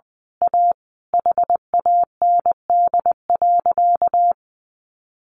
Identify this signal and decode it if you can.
Morse code, heard as a keyed tone, about 19 wpm, 700 Hz. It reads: A HAND.
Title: A Sound Of Thunder 20wpm